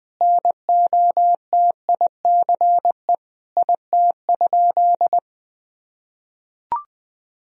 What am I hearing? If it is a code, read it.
Morse code, heard as a keyed tone, about 21 wpm, 700 Hz. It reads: NOTICE IT?
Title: Sentences from Top 1000 Words - Review Part 2 - 20wpm